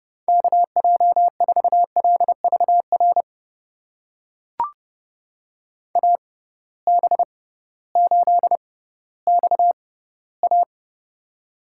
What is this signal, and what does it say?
Morse code, heard as a keyed tone, about 30 wpm, spaced as for 16 wpm, 700 Hz. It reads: KJ4LVR AB8XA
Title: USA Call Signs - 6x Character Spacing - 30wpm